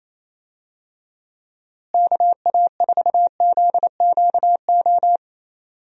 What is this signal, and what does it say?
Morse code, heard as a keyed tone, about 28 wpm, 700 Hz. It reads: KA4ZQO
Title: USA Call Signs - 6 Characters - Repeated 3x - 28wpm